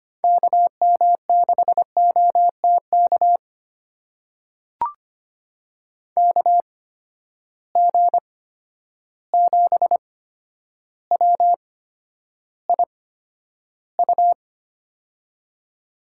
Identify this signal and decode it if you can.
Morse code, heard as a keyed tone, about 25 wpm, spaced as for 11 wpm, 700 Hz. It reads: KM6OTK KG7WIU
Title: USA Call Signs - 6 Characters - 8x Character Spacing - 25wpm